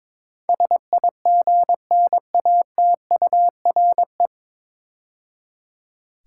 Morse code, heard as a keyed tone, about 22 wpm, 700 Hz. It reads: SIGNATURE
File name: 9 Letter Words - First 3 Letters Given - 22wpm